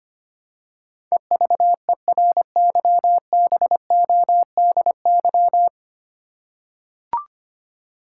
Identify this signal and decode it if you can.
Morse code, heard as a keyed tone, about 25 wpm, 700 Hz. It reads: EVERYBODY
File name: 9 Letter Words - First 3 Letters Given - 25wpm